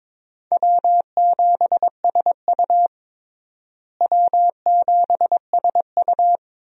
Morse code, heard as a keyed tone, about 22 wpm, 700 Hz. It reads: W7SU W7SU
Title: USA Call Signs - 4 Characters - Repeated 3x - 22wpm